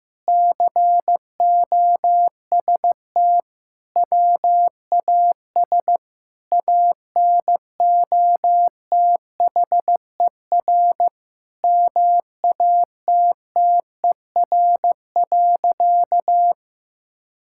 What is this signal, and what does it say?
Morse code, heard as a keyed tone, about 15 wpm, 700 Hz. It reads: COST WAS ANOTHER MATTER.
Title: Sentences from Top 1000 Words - Review Part 5 - 15wpm